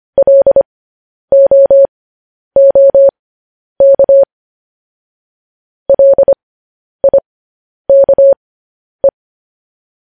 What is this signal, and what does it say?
Morse code, heard as a keyed tone, about 25 wpm, spaced as for 10 wpm, 550 Hz. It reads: LOOK LIKE
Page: News Headlines in Morse Code at 10 WPM